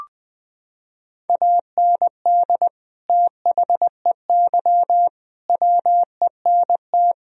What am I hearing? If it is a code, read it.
Morse code, heard as a keyed tone, about 20 wpm, 700 Hz. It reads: AND THEY WENT
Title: Sentences from Top 1000 Words - Review Part 2 - 20wpm